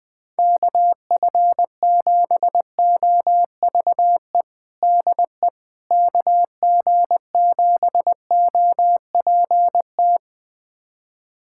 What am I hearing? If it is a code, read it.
Morse code, heard as a keyed tone, about 20 wpm, 700 Hz. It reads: KF7OVE DE KG7OPT
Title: USA Callsign DE Callsign - 6 Characters - 20wpm